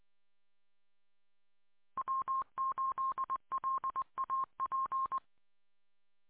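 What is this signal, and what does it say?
Morse code, heard as a keyed tone, about 20 wpm, 1060 Hz. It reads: W8LAP
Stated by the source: Morse code on a 146.620 repeater.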